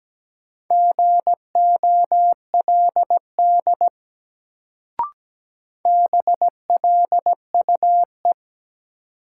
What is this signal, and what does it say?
Morse code, heard as a keyed tone, about 17 wpm, 700 Hz. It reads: GOLD BLUE
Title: Sentences from Top 1000 Words - Review Part 3 - New Word Drill - 17wpm